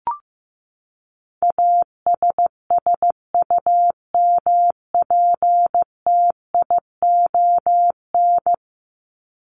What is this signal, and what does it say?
Morse code, heard as a keyed tone, about 15 wpm, 700 Hz. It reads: ASSUMPTION
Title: Top 1751-2000 Words 15wpm